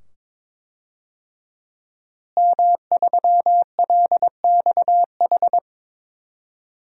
Morse code, heard as a keyed tone, about 22 wpm, 700 Hz. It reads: M3LXH
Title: DXCC-100 Call Signs - 5 Characters - Repeated 2x - 22wpm